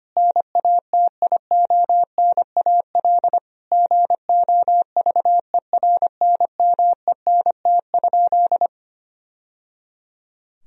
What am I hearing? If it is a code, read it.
Morse code, heard as a keyed tone, about 25 wpm, 700 Hz. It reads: NATIONAL GOVERNMENT?
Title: Sentences from Top 1000 Words - Review Part 4 - 25wpm